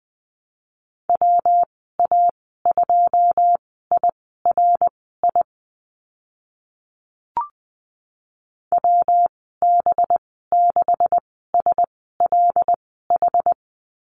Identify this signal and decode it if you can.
Morse code, heard as a keyed tone, about 19 wpm, spaced as for 14 wpm, 700 Hz. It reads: WA2IRI WB6SLH
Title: 186 Lesson 30g - Letter b - Call Signs 20wpm